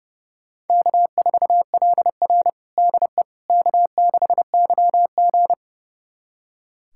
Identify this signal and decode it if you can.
Morse code, heard as a keyed tone, about 30 wpm, 700 Hz. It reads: K4LR DE K6YG
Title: USA Callsign DE Callsign - 4 Characters - 30wpm